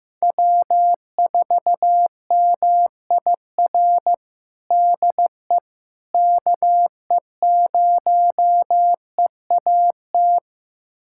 Morse code, heard as a keyed tone, about 15 wpm, 700 Hz. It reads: W4MIR DE KE0EAT
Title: USA Callsign DE Callsign 15wpm